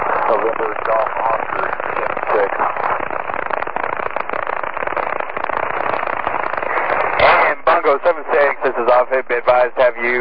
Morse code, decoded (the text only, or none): none